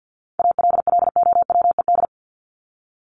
Morse code, heard as a keyed tone, about 50 wpm, 700 Hz. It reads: ALLOWED
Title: ED Suffix - Top 100 Words with Base Words 50wpm